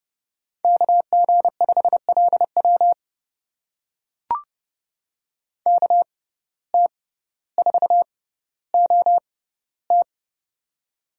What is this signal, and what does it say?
Morse code, heard as a keyed tone, about 30 wpm, spaced as for 15 wpm, 700 Hz. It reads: KG5LW KT4OT
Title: USA Call Signs - 5 Characters - 6x Character Spacing - 30wpm